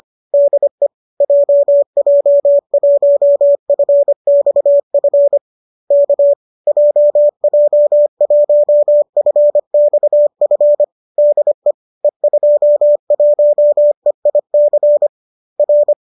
Morse code, heard as a keyed tone, about 25 wpm, 590 Hz. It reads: DE JJ1FXF K JJ1FXF DE E21EIC R